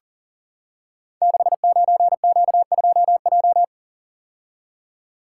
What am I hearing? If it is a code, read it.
Morse code, heard as a keyed tone, about 40 wpm, 700 Hz. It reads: 69Q2J
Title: Code Groups - 5 Letter:Numbers - 40wpm